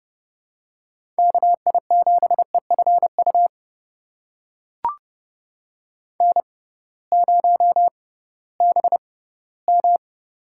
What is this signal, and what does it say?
Morse code, heard as a keyed tone, about 32 wpm, spaced as for 18 wpm, 700 Hz. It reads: KI7EFU N0BM
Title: USA Call Signs - 6x Character Spacing - 30wpm